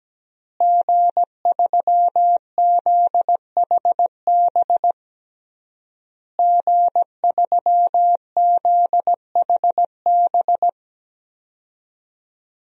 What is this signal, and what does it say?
Morse code, heard as a keyed tone, about 17 wpm, 700 Hz. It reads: G3ZHB G3ZHB
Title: DXCC-100 Call Signs - 5 Characters - Repeated 3x - 17wpm